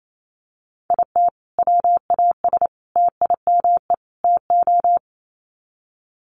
Morse code, heard as a keyed tone, about 28 wpm, 700 Hz. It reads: IT WAS TIME TO
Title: Sets of 4 Words 28wpm